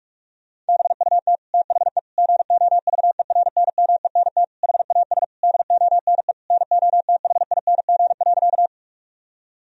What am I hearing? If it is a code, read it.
Morse code, heard as a keyed tone, about 46 wpm, 700 Hz. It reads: BUT THE GOVERNMENT HAS DONE NOTHING.
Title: Sentences from Top 1000 Words - Review Part 3 - 45wpm